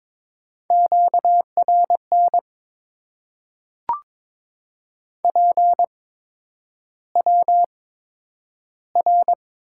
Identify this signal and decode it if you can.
Morse code, heard as a keyed tone, about 22 wpm, spaced as for 9 wpm, 700 Hz. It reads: QRN PWR
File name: Top 90 QSO Elements - 8x Character Spacing - 22wpm